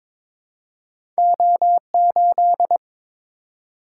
Morse code, heard as a keyed tone, about 22 wpm, 700 Hz. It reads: O8
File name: practice / 22wpm / Two Letter-Number Combinations - 22wpm